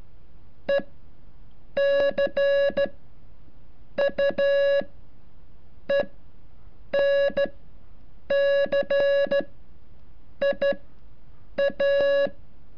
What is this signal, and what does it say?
Morse code, heard as a keyed tone, about 13 wpm, spaced as for 7 wpm, 560 Hz. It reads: ECUENCIA
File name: curso / ejercicio 33